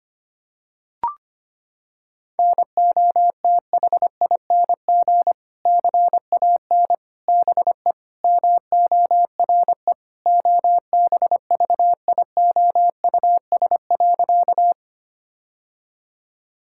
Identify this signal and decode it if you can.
Morse code, heard as a keyed tone, about 25 wpm, 700 Hz. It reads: NOTHING CAN BE MORE OBVIOUS.
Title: Sentences from Top 1600 Words 25wpm